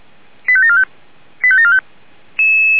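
Five-tone selective call (ZVEI), heard as quick runs of five tones, tones of about 70 ms. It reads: 85745 85745
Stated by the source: ZVEI five-tone call 85730 received on an SDR.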